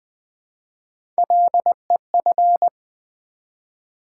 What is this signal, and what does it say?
Morse code, heard as a keyed tone, about 21 wpm, 700 Hz. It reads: LEF
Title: Build A Word - 1 Letter At A Time - 4-Letter Words - 20wpm